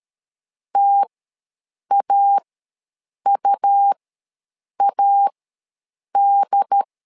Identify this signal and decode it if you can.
Morse code, heard as a keyed tone, about 13 wpm, spaced as for 7 wpm, 800 Hz. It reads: TAUAD
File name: LEZIONI / 3 MWRO / 3 M ws 12wpm 1745